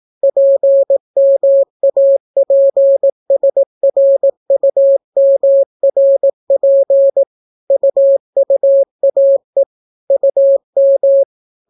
Morse code, heard as a keyed tone, about 18 wpm, 550 Hz. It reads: PMAPSRUMRP UUAE UM